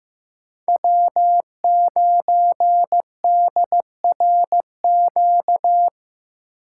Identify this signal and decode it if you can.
Morse code, heard as a keyed tone, about 18 wpm, 700 Hz. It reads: W9DRQ
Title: Speed Racing - USA 1x3 Call Signs 15wpm